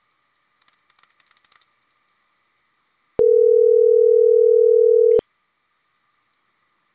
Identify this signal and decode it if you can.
Ringback tone (North America), 440 + 480 Hz.